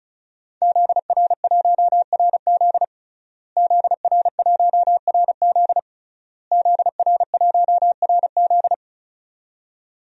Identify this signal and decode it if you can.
Morse code, heard as a keyed tone, about 36 wpm, 700 Hz. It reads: ZR1RZ ZR1RZ ZR1RZ
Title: DXCC-100 Call Signs - 5 Characters - Repeated 3x - 35wpm